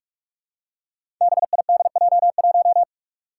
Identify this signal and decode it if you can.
Morse code, heard as a keyed tone, about 42 wpm, 700 Hz. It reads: BIDJ1
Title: Five Challenging Letter-Number Combinations 45wpm